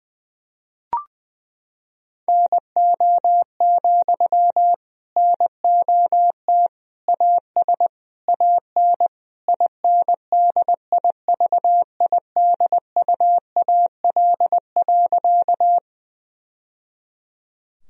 Morse code, heard as a keyed tone, about 20 wpm, 700 Hz. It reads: NO, NOT AS AN INDIVIDUAL.
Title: Sentences from Top 1000 Words - Review Part 4 - 20wpm